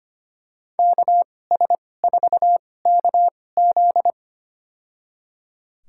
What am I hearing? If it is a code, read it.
Morse code, heard as a keyed tone, about 25 wpm, spaced as for 18 wpm, 700 Hz. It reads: KS4KZ
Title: USA Call Signs - 5 Characters - 2x Character Spacing - 25wpm